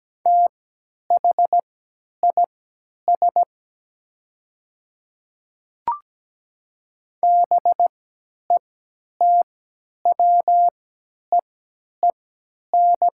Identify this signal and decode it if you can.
Morse code, heard as a keyed tone, about 17 wpm, spaced as for 10 wpm, 700 Hz. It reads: THIS BETWEEN
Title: 3x Character Spacing - Top 300 Words - 17wpm